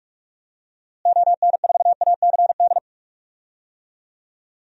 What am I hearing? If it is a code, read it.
Morse code, heard as a keyed tone, about 46 wpm, 700 Hz. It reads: ON4ACD
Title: DXCC-100 Call Signs - 6 Characters - Repeated 2x - 45wpm